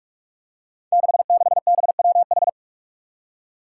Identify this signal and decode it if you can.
Morse code, heard as a keyed tone, about 44 wpm, 700 Hz. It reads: 66BWH
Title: Five Challenging Letter-Number Combinations 45wpm